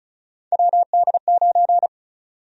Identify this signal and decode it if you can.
Morse code, heard as a keyed tone, about 34 wpm, 700 Hz. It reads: WD9
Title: USA Call Sign Prefixes - 35wpm